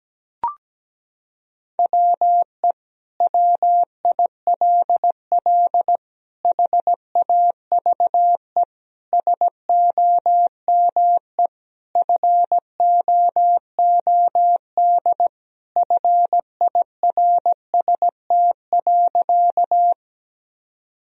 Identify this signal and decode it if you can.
Morse code, heard as a keyed tone, about 17 wpm, 700 Hz. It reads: WE WILL HAVE SOME FOOD FIRST.
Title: Sentences from Top 1000 Words - Review Part 1 - 17wpm